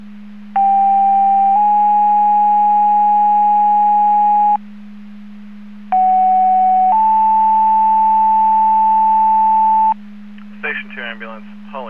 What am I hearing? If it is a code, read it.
Two-tone page, heard as a one-second tone, then a three-second tone, about 790 then 830 Hz, about 750 then 880 Hz, with speech. It is heard from 0.6 to 9.9 s.